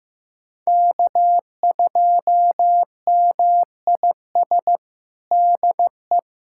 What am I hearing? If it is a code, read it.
Morse code, heard as a keyed tone, about 15 wpm, 700 Hz. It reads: K2MIS DE
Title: USA Callsign DE Callsign - 5 Characters - 15wpm